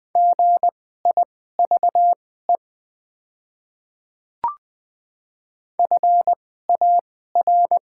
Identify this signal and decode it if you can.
Morse code, heard as a keyed tone, about 20 wpm, spaced as for 14 wpm, 700 Hz. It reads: GIVE FAR
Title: 2x Character Spacing - Top 300 Words - 20wpm